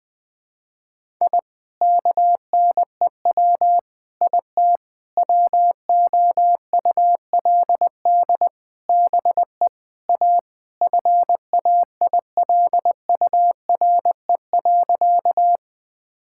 Morse code, heard as a keyed tone, about 20 wpm, 700 Hz. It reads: I KNEW IT WOULD BE A FAILURE.
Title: Sentences from Top 1000 Words - Review Part 6 - 20wpm